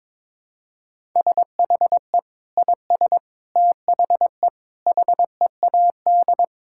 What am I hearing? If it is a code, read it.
Morse code, heard as a keyed tone, about 22 wpm, 700 Hz. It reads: SHE IS THE HEAD
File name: practice / 22wpm / Sentences with 4-Word Phrases - 22wpm